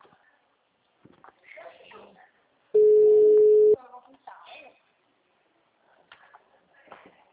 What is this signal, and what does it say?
Ringback tone (Europe, 425 Hz), with speech.